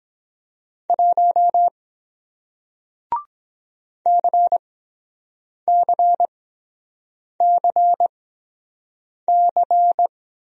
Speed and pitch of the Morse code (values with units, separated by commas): 23 wpm, spaced as for 7 wpm, 700 Hz